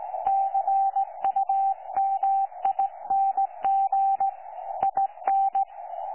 Morse code, heard as a keyed tone, about 17 wpm, 780 Hz. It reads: CUMING IN
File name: 12 39Z 18088.2kHz utah aa7fv k9tjl